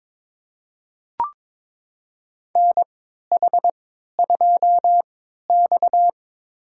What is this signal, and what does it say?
Morse code, heard as a keyed tone, about 22 wpm, spaced as for 13 wpm, 700 Hz. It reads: NH2X
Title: USA Call Signs - 4 Characters - 3x Character Spacing - 22wpm